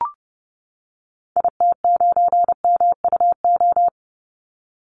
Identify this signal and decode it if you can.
Morse code, heard as a keyed tone, about 30 wpm, 700 Hz. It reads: IT9MUO
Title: Call Signs - 30wpm